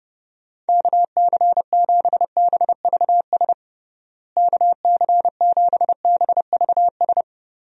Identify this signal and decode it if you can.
Morse code, heard as a keyed tone, about 30 wpm, 700 Hz. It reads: KC7BVS KC7BVS
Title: USA Call Signs - 6 Characters - Repeated 2x - 30wpm